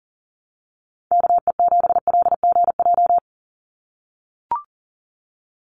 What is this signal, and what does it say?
Morse code, heard as a keyed tone, about 40 wpm, 700 Hz. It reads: KE7LGJ
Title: USA Call Signs 40wpm